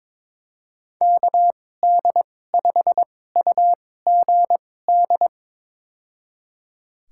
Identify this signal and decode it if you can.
Morse code, heard as a keyed tone, about 23 wpm, spaced as for 16 wpm, 700 Hz. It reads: KD5UGD
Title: USA Call Signs - 6 Characters - 2x Character Spacing - 22wpm